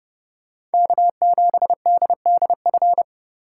Morse code, heard as a keyed tone, about 31 wpm, 700 Hz. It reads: K7DDF